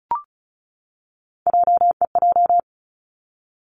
Morse code, heard as a keyed tone, about 36 wpm, 700 Hz. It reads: JEJ
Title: Three Challenging Letter-Number Combinations 35wpm